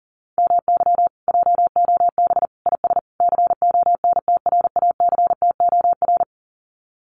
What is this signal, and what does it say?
Morse code, heard as a keyed tone, about 40 wpm, 700 Hz. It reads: MY JOB IS CONTRACTOR
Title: Common QSO Phrases Encore 40wpm